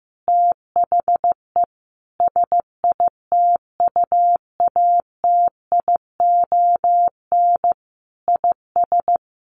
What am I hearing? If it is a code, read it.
Morse code, heard as a keyed tone, about 15 wpm, 700 Hz. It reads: THE SITUATION IS
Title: Sentences from Top 1200 Words 15wpm